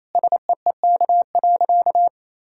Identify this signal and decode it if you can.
Morse code, heard as a keyed tone, about 27 wpm, 700 Hz. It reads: SEEK.
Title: Sentences from Top 1600 Words 28wpm